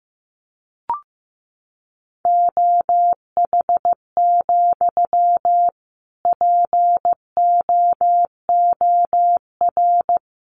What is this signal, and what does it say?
Morse code, heard as a keyed tone, about 15 wpm, 700 Hz. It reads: OH, POOR
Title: Sentences from Top 1000 Words 15wpm